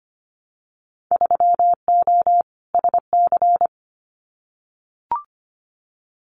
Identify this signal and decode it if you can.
Morse code, heard as a keyed tone, about 25 wpm, 700 Hz. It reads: 3O SC
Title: Winter Field Day Exchanges 25wpm